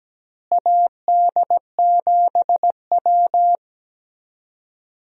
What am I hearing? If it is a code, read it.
Morse code, heard as a keyed tone, about 17 wpm, 700 Hz. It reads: AD7W
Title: USA Call Signs - 4 Characters - Repeated 2x - 17wpm